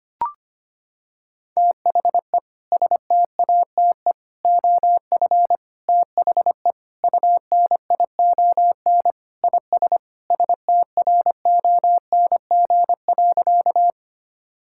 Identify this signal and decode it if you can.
Morse code, heard as a keyed tone, about 25 wpm, 700 Hz. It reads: THE STATE OF THE UNION IS STRONG.
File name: practice / 25wpm / Sentences with 3-Word Phrases - 25wpm